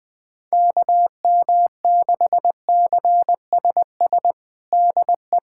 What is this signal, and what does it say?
Morse code, heard as a keyed tone, about 20 wpm, 700 Hz. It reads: KM6CSS DE
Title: USA Callsign DE Callsign 20wpm